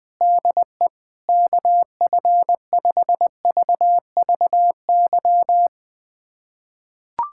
Morse code, heard as a keyed tone, about 21 wpm, 700 Hz. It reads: DE KF5VVY
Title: USA Callsign DE Callsign - 6 Characters - 20wpm